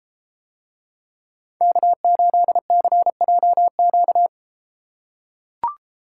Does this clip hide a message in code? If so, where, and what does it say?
Yes: Morse code from 1.6 to 4.3 s, heard as a keyed tone, about 30 wpm, 700 Hz. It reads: K8CJQ